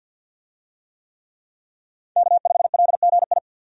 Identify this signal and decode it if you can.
Morse code, heard as a keyed tone, about 52 wpm, 700 Hz. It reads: K5LGI
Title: USA Call Signs - 5 Characters - Repeated 3x - 50wpm